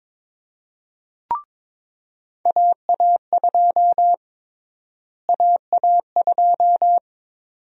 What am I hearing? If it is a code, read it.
Morse code, heard as a keyed tone, about 22 wpm, 700 Hz. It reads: AA2 AA2
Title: USA Call Sign Prefixes - Repeated 2x - 22wpm